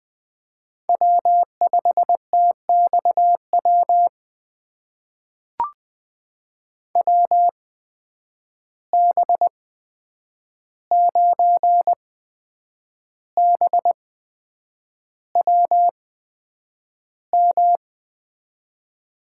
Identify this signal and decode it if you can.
Morse code, heard as a keyed tone, about 20 wpm, spaced as for 8 wpm, 700 Hz. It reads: W5TXW WB9BWM
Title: USA Call Signs - 8x Character Spacing - 20wpm